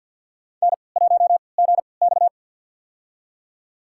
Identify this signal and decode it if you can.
Morse code, heard as a keyed tone, about 53 wpm, spaced as for 29 wpm, 700 Hz. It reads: N1GX
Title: USA Call Signs - 3x Character Spacing - 50wpm